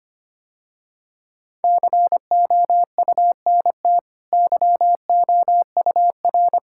Morse code, heard as a keyed tone, about 25 wpm, 700 Hz. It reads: COUNT YOUR
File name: Easy Sentences with Hints - 25wpm